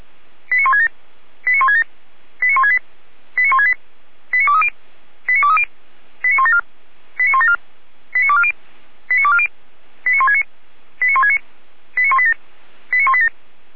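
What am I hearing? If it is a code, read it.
Five-tone selective call (ZVEI), heard as quick runs of five tones, tones of about 80 ms. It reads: 89167 89167 89168 89168 89230 89230 89174 89174 89240 89240 89169 89169 89187 89187
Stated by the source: ZVEI five-tone selective call received by SDR.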